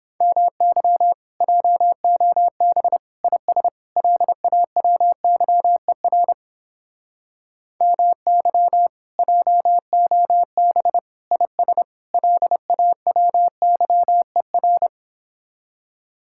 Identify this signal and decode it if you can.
Morse code, heard as a keyed tone, about 29 wpm, 700 Hz. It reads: MY JOB IS LAWYER MY JOB IS LAWYER